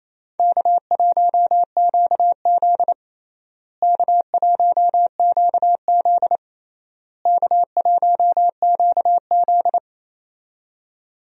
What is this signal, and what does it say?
Morse code, heard as a keyed tone, about 27 wpm, 700 Hz. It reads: K1QZ K1QZ K1QZ
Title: USA Call Signs - 4 Characters - Repeated 3x - 28wpm